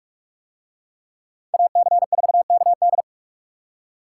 Morse code, heard as a keyed tone, about 41 wpm, 700 Hz. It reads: AC4KD